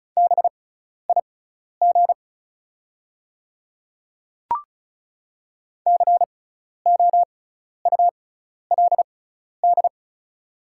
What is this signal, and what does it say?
Morse code, heard as a keyed tone, about 34 wpm, spaced as for 12 wpm, 700 Hz. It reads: BIG COULD
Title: 6x Character Spacing - Top 200 Words - 35wpm